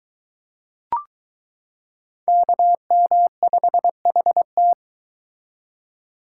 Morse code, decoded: KM5HT

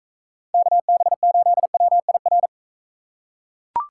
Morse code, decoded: KB8WIR

